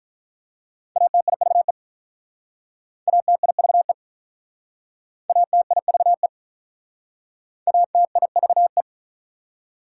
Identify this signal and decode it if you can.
Morse code, heard as a keyed tone, about 44 wpm, 700 Hz. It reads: ATIVE ATIVE ATIVE ATIVE
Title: Speed-Racing Prefixes and Suffixes 35wpm